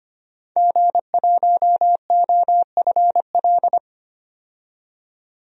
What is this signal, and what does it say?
Morse code, heard as a keyed tone, about 25 wpm, 700 Hz. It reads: G1OFL
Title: DXCC-100 Call Signs - 5 Characters - 25wpm